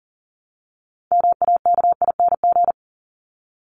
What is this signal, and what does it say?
Morse code, heard as a keyed tone, about 40 wpm, 700 Hz. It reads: MAKING
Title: ING Suffix - Top 100 Words with Base Words 40wpm